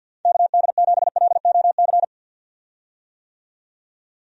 Morse code, heard as a keyed tone, about 47 wpm, 700 Hz. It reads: KD7LOC
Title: USA Call Signs - 6 Characters - 50wpm